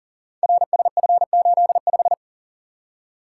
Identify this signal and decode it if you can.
Morse code, heard as a keyed tone, about 38 wpm, 700 Hz. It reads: RSF85
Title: Code Groups - 5 Letter:Numbers - 40wpm